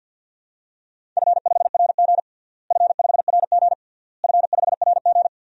Morse code, heard as a keyed tone, about 52 wpm, 700 Hz. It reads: F5RG F5RG F5RG
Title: DXCC-100 Call Signs - 4 Characters - Repeated 3x - 50wpm